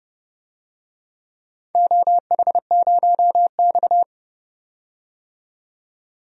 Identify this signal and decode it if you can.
Morse code, heard as a keyed tone, about 30 wpm, 700 Hz. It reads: OH0X
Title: DXCC-100 Call Signs - 4 Characters - 30wpm